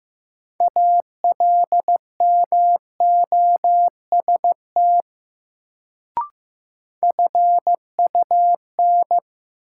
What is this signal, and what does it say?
Morse code, heard as a keyed tone, about 15 wpm, 700 Hz. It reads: ALMOST FUN
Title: Sentences from Top 700 Words - New Word Drill - 15wpm